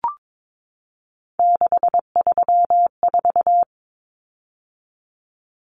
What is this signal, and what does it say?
Morse code, heard as a keyed tone, about 22 wpm, 700 Hz. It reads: 634
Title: Common Cut Numbers - 3-Digit 22wpm